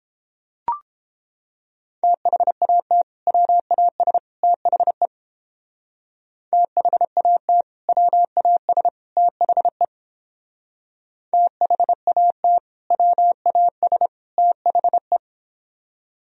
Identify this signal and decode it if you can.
Morse code, heard as a keyed tone, about 31 wpm, 700 Hz. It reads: THAT WAS THE THAT WAS THE THAT WAS THE